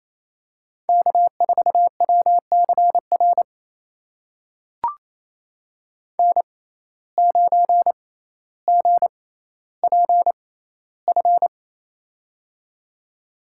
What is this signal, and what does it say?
Morse code, heard as a keyed tone, about 28 wpm, spaced as for 14 wpm, 700 Hz. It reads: K4WCR N9GPF
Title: USA Call Signs - 5 Characters - 6x Character Spacing - 28wpm